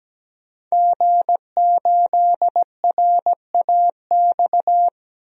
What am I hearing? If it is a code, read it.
Morse code, heard as a keyed tone, about 17 wpm, 700 Hz. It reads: G8RAX
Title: DXCC-100 Call Signs - 5 Characters - 17wpm